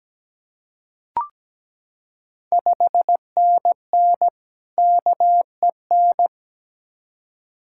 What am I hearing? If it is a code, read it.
Morse code, heard as a keyed tone, about 17 wpm, 700 Hz. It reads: 5NN KEN
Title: Ontario QSO Party Exchange 17wpm